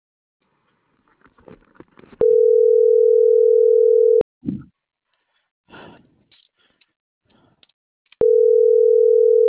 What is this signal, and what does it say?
Ringback tone (North America), 440 + 480 Hz.